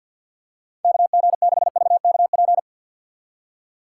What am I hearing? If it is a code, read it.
Morse code, heard as a keyed tone, about 48 wpm, 700 Hz. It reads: KG6VKP
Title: USA Call Signs - 6 Characters - 3x Character Spacing - 50wpm